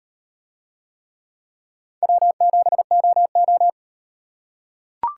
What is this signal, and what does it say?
Morse code, heard as a keyed tone, about 38 wpm, 700 Hz. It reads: W7OO